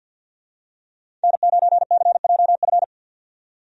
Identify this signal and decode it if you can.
Morse code, heard as a keyed tone, about 49 wpm, 700 Hz. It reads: N9CJF
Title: USA Call Signs - 5 Characters - 50wpm